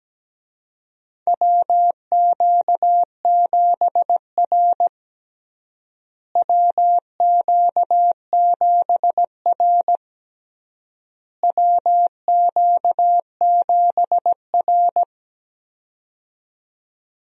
Morse code, heard as a keyed tone, about 17 wpm, 700 Hz. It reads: WQ7R WQ7R WQ7R
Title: USA Call Signs - 4 Characters - Repeated 3x - 17wpm